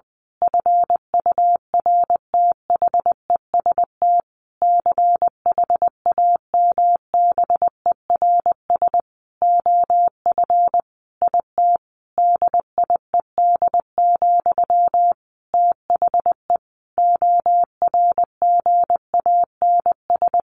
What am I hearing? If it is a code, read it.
Morse code, heard as a keyed tone, about 20 wpm, 700 Hz. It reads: FURTHEST CHAMBERS OF IT DIED, THE ORGANS
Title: A Sound Of Thunder 20wpm